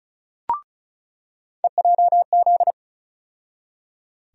Morse code, heard as a keyed tone, about 34 wpm, 700 Hz. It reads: EJZ
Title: USA Call Sign Postfixes - 3 Letters 35wpm